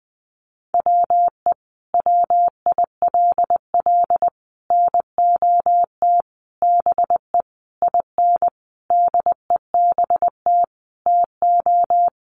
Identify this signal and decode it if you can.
Morse code, heard as a keyed tone, about 20 wpm, 700 Hz. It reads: WE WILL NOT BE IN DEBT TO
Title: Sentences from Top 1300 Words 20wpm